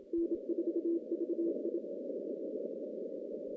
Morse code, heard as a keyed tone, about 25 wpm, 340 Hz, with noise. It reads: N4VI
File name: contests / CW-SS-k5tr- -0700